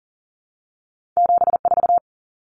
Morse code, heard as a keyed tone, about 40 wpm, 700 Hz. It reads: 74
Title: Common Cut Numbers - 2-Digit 40wpm